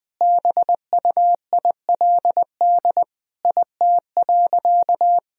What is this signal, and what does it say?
Morse code, heard as a keyed tone, about 21 wpm, 700 Hz. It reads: BUILD IT.
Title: Sentences from Top 1000 Words - Review Part 1 - 20wpm